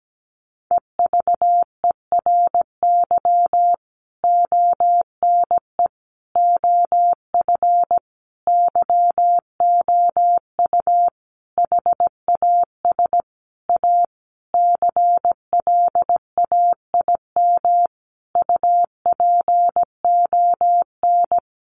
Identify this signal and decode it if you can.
Morse code, heard as a keyed tone, about 17 wpm, 700 Hz. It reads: EVERY ONE OF YOU HAS A CLAIM UPON
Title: Sentences from Top 1500 Words 17wpm